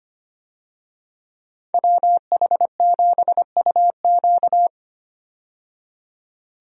Morse code, heard as a keyed tone, about 25 wpm, 690 Hz. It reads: WH7UQ
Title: USA Call Signs - 5 Characters - 25wpm